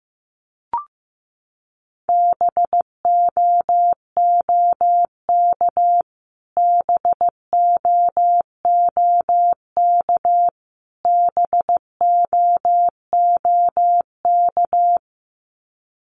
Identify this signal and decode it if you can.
Morse code, heard as a keyed tone, about 15 wpm, 700 Hz. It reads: BOOK BOOK BOOK
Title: Top 500 Words Repeated 15wpm